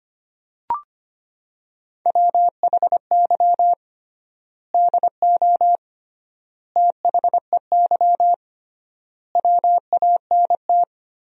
Morse code, heard as a keyed tone, about 25 wpm, 700 Hz. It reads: WHY DO THEY WANT
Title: 3x Word-Spacing - Sentences from Top 400 Words - 25wpm